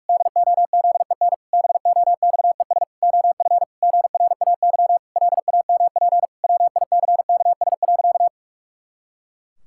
Morse code, heard as a keyed tone, about 46 wpm, 700 Hz. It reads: DOZEN BOXES OF GRAY LAMP WICKS.
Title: Pangram Challenge v2 - 45wpm